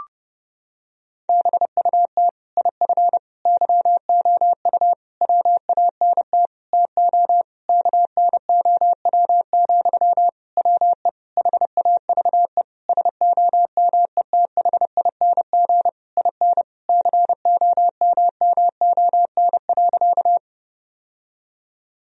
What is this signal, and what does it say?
Morse code, heard as a keyed tone, about 32 wpm, 700 Hz. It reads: BUT IF YOU WANT TO KNOW, WE HAVE SOMETHING IN COMMON.
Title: Sentences from Top 1000 Words - Review Part 3 - 30wpm